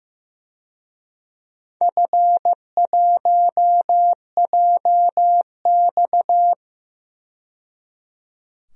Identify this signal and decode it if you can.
Morse code, heard as a keyed tone, about 15 wpm, 700 Hz. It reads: F1JX
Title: DXCC-100 Call Signs - 4 Characters - 15wpm